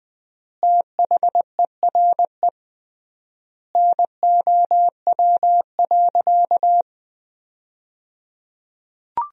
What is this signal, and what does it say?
Morse code, heard as a keyed tone, about 20 wpm, 700 Hz. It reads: THERE NOW.
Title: 3x Word-Spacing - Sentences from Top 200 Words - 20wpm